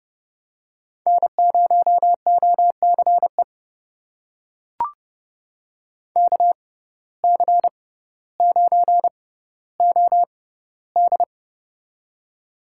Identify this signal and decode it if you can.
Morse code, heard as a keyed tone, about 30 wpm, spaced as for 15 wpm, 700 Hz. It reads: N0OCE KC9OD
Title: USA Call Signs - 5 Characters - 6x Character Spacing - 30wpm